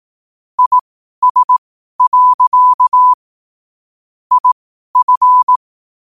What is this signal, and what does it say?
Morse code, heard as a keyed tone, about 18 wpm, spaced as for 12 wpm, 1000 Hz. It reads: IS. IF
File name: 2026-01-07-12wpm